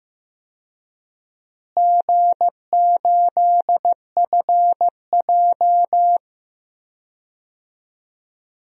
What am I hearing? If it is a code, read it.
Morse code, heard as a keyed tone, about 15 wpm, 700 Hz. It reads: G8FJ
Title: DXCC-100 Call Signs - 4 Characters - Repeated 3x - 15wpm